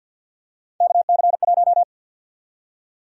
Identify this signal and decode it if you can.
Morse code, heard as a keyed tone, about 53 wpm, 700 Hz. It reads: KC1
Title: USA Call Sign Prefixes - Repeated 2x - 50wpm